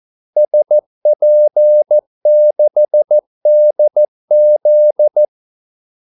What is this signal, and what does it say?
Morse code, heard as a keyed tone, about 14 wpm, 600 Hz. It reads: SP6DZ
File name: Callsigns-14wpm